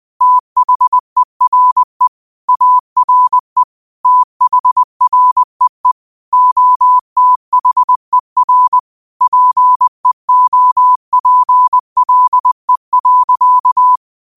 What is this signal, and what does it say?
Morse code, heard as a keyed tone, about 20 wpm, 1000 Hz. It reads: THERE ARE THREE OTHER PEOPLE.